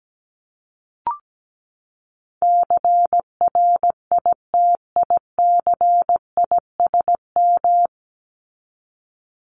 Morse code, heard as a keyed tone, about 17 wpm, 700 Hz. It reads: CRITICISM
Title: Top 1001-1250 Words 17wpm